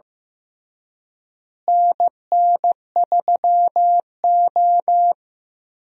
Morse code, heard as a keyed tone, about 15 wpm, 700 Hz. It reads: NN3O
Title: USA Call Signs - 4 Characters - Repeated 3x - 15wpm